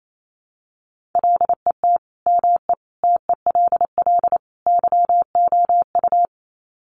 Morse code, heard as a keyed tone, about 28 wpm, 700 Hz. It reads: LET ME TELL YOU
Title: Sets of 4 Words 28wpm